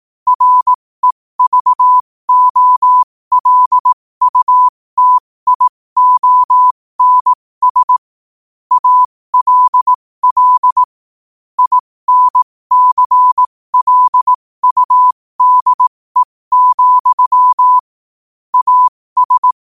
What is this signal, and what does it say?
Morse code, heard as a keyed tone, about 18 wpm, spaced as for 15 wpm, 1000 Hz. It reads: REVOLUTIONS ALL INCLUDE, AS
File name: qotdfiles / 15wpm